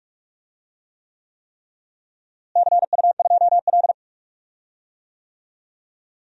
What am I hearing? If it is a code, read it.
Morse code, heard as a keyed tone, about 43 wpm, 700 Hz. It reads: CU2L
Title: DXCC-100 Call Signs - 4 Characters - Repeated 3x - 45wpm